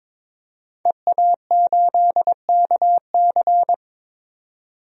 Morse code, heard as a keyed tone, about 22 wpm, 700 Hz. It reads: EA8KC
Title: DXCC-100 Call Signs - 5 Characters - Repeated 2x - 22wpm